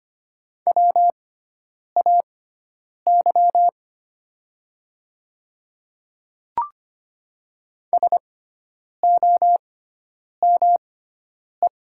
Morse code, heard as a keyed tone, about 25 wpm, spaced as for 9 wpm, 700 Hz. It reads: WAY SOME
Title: 6x Character Spacing - Top 100 Words - 25wpm